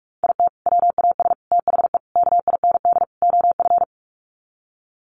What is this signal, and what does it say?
Morse code, heard as a keyed tone, about 44 wpm, 700 Hz. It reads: IT WAS THE KIND OF
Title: Sets of 5 Words Encore2 45wpm